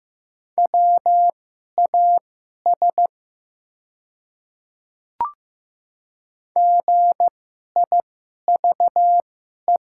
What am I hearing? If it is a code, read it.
Morse code, heard as a keyed tone, about 15 wpm, spaced as for 11 wpm, 700 Hz. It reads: WAS GIVE
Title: 2x Character Spacing - Top 200 Words - 15wpm